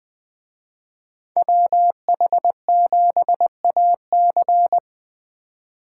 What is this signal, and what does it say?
Morse code, heard as a keyed tone, about 20 wpm, 700 Hz. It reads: WH7AC
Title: Speed Racing - USA Call Signs 17wpm